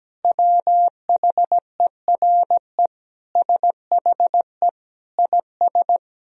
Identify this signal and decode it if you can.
Morse code, heard as a keyed tone, about 17 wpm, 700 Hz. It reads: WHERE SHE IS
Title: Sentences from Top 1000 Words - Review Part 2 - 17wpm